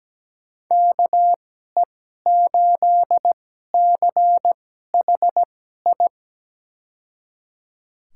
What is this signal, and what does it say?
Morse code, heard as a keyed tone, about 17 wpm, spaced as for 12 wpm, 700 Hz. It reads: KE8CHI
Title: USA Call Signs - 6 Characters - 2x Character Spacing - 17wpm